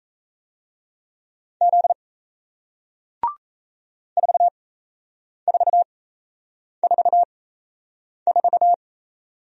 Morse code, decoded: Z 4444

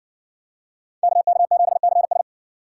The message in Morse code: KK7CS

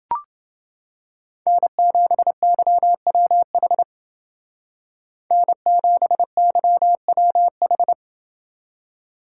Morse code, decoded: N7YWH N7YWH

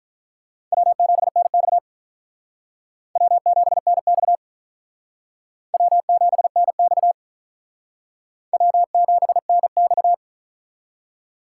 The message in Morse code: W7NX W7NX W7NX W7NX

